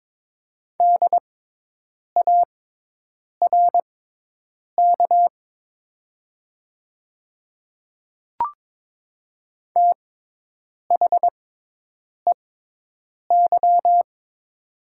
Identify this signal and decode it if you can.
Morse code, heard as a keyed tone, about 22 wpm, spaced as for 8 wpm, 700 Hz. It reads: DARK THEY
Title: 6x Character Spacing - Top 500 Words - 22wpm